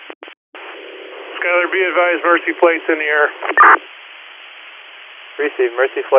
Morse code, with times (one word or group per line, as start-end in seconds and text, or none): none